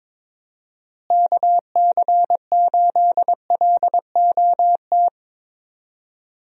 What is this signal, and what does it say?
Morse code, heard as a keyed tone, about 22 wpm, 700 Hz. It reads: KC8LOT
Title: USA Call Signs - 6 Characters - Repeated 2x - 22wpm